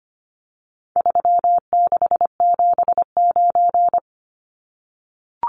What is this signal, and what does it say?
Morse code, heard as a keyed tone, about 25 wpm, 700 Hz. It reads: 3679
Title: Numbers - 4-Digit 25wpm